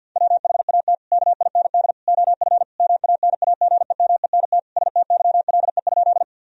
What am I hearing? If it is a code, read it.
Morse code, heard as a keyed tone, about 50 wpm, 700 Hz. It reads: WHAT KIND OF MANAGEMENT STYLE?
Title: Sentences from Top 1000 Words - Review Part 4 - 50wpm